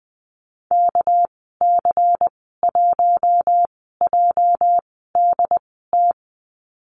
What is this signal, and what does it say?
Morse code, heard as a keyed tone, about 20 wpm, spaced as for 15 wpm, 700 Hz. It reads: KC1JDT
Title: 207 Lesson 33g - Letter j - Call Signs 20wpm